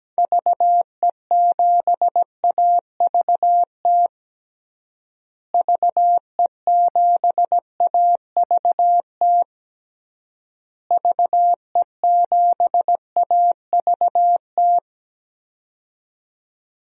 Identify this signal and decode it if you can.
Morse code, heard as a keyed tone, about 17 wpm, 700 Hz. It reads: VE7AVT VE7AVT VE7AVT
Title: DXCC-100 Call Signs - 6 Characters - Repeated 3x - 17wpm